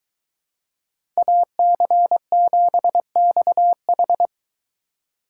Morse code, heard as a keyed tone, about 23 wpm, 700 Hz. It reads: AC7XH